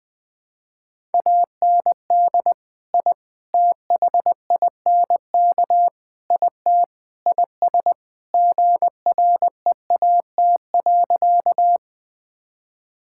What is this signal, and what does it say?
Morse code, heard as a keyed tone, about 20 wpm, 700 Hz. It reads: AND I THINK IT IS GREAT.
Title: Sentences with 4-Word Phrases - Spoken Beginning - 20wpm